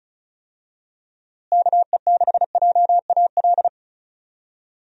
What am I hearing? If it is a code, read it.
Morse code, heard as a keyed tone, about 34 wpm, 700 Hz. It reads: KE6JAL
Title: USA Call Signs - 6 Characters - Repeated 2x - 35wpm